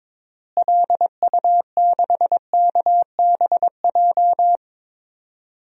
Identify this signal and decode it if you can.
Morse code, heard as a keyed tone, about 22 wpm, 700 Hz. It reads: LU6KBJ
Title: DXCC-100 Call Signs - 6 Characters - Repeated 3x - 22wpm